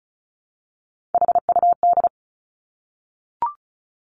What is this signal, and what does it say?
Morse code, heard as a keyed tone, about 36 wpm, 700 Hz. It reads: HUD
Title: Three Challenging Letter-Number Combinations 35wpm